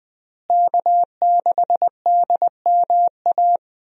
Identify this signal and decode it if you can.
Morse code, heard as a keyed tone, about 20 wpm, 700 Hz. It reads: K6DMA
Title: USA Call Signs - 5 Characters - Repeated 2x - 20wpm